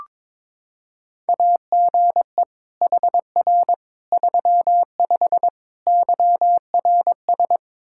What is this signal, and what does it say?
Morse code, heard as a keyed tone, about 22 wpm, 700 Hz. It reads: AGE HR 35 YRS
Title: Practice QSO - Exchange 3 - More Info - 22wpm